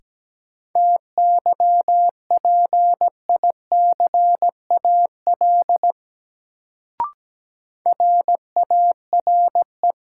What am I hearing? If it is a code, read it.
Morse code, heard as a keyed tone, about 17 wpm, 700 Hz. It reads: TYPICAL RARE
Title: Sentences from Top 1100 Words - New Word Drill - 17wpm